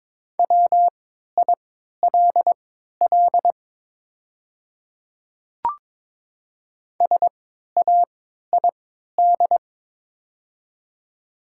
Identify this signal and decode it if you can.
Morse code, heard as a keyed tone, about 22 wpm, spaced as for 13 wpm, 700 Hz. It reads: WILL SAID
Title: 3x Character Spacing - Top 200 Words - 22wpm